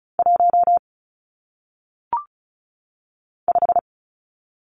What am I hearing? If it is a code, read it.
Morse code, heard as a keyed tone, about 36 wpm, 700 Hz. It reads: 1 5
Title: Numbers - 1-Digit 35wpm